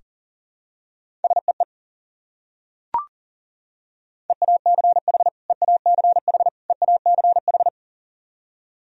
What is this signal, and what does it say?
Morse code, heard as a keyed tone, about 40 wpm, 700 Hz. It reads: SEE EACH EACH EACH
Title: Top 100 Words Repeated 40wpm